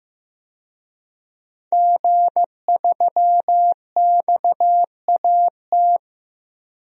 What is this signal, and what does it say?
Morse code, heard as a keyed tone, about 15 wpm, 700 Hz. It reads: G3XAT
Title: DXCC-100 Call Signs - 5 Characters - 15wpm